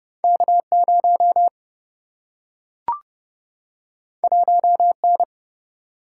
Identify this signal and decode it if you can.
Morse code, heard as a keyed tone, about 30 wpm, 700 Hz. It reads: K0 1N
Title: Single-Letter Single-Number 30wpm